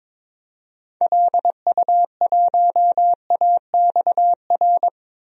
Morse code, heard as a keyed tone, about 22 wpm, 700 Hz. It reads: LU1AXR
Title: DXCC-100 Call Signs - 6 Characters - 22wpm